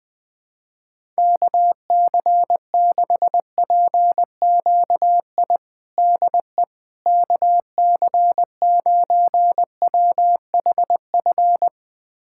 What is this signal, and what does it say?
Morse code, heard as a keyed tone, about 20 wpm, 700 Hz. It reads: KC6PQI DE KC9WHF
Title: USA Callsign DE Callsign - 6 Characters - 20wpm